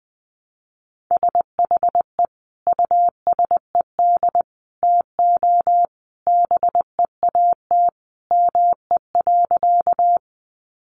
Morse code, heard as a keyed tone, about 20 wpm, 700 Hz. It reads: SHE USED TO BEAT ME.
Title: Sentences from Top 1300 Words 20wpm